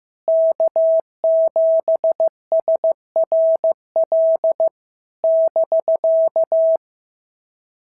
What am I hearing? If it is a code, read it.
Morse code, heard as a keyed tone, about 15 wpm, 650 Hz. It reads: K7SRL <BK>
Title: SOTA QSO Practice 15wpm